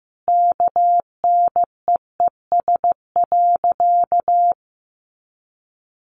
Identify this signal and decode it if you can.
Morse code, heard as a keyed tone, about 15 wpm, 700 Hz. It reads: KNEES.
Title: A Sound Of Thunder 15wpm